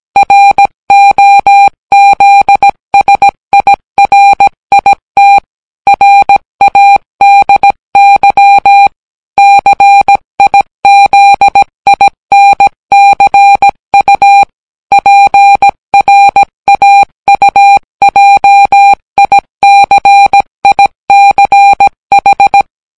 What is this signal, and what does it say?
Morse code, heard as a keyed tone, about 17 wpm, 800 Hz. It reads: ROZSIRIT RADY CIZINCU PRAUJICICH